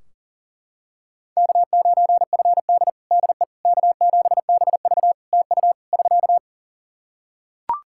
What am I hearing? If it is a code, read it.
Morse code, heard as a keyed tone, about 37 wpm, 700 Hz. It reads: K9FD DE K7BV TU <SK>
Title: Practice QSO - Exchange 4 - Ending - 40wpm